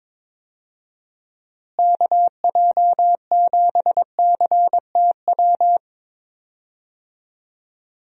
Morse code, heard as a keyed tone, about 22 wpm, 700 Hz. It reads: KJ7CTW
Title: USA Call Signs - 6 Characters - Repeated 3x - 22wpm